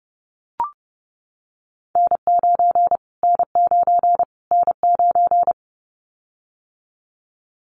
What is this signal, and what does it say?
Morse code, heard as a keyed tone, about 30 wpm, 690 Hz. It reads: N9 N9 N9
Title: Callsign Prefixes for Top 25 DXCC - Repeated 3x - 30wpm